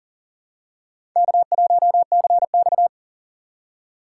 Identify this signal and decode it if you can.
Morse code, heard as a keyed tone, about 40 wpm, 700 Hz. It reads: K1CX
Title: K1USN SST QSO Practice - 40wpm